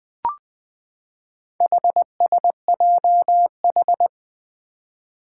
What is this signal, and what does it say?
Morse code, heard as a keyed tone, about 20 wpm, 700 Hz. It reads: HSJH
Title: Four Challenging Letter-Number Combinations 20wpm